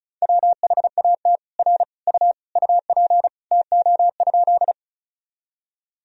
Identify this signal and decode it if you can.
Morse code, heard as a keyed tone, about 34 wpm, 700 Hz. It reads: WHAT R U UP TO?
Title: Sentences from Top 100 Words and Abbreviations 35wpm